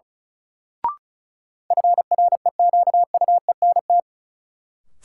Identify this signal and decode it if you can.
Morse code, heard as a keyed tone, about 36 wpm, 700 Hz. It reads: FREQUENT
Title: Sentences from Top 1400 Words - New Word Drill - 35wpm